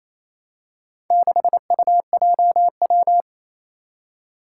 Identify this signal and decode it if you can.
Morse code, heard as a keyed tone, about 27 wpm, 700 Hz. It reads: 6UJW
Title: Four Challenging Letter-Number Combinations 28wpm